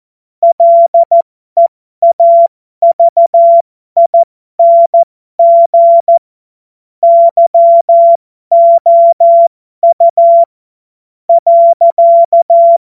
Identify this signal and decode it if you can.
Morse code, heard as a keyed tone, about 14 wpm, spaced as for 12 wpm, 680 Hz. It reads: LEAVING YOU .
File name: Morsekurs / Witze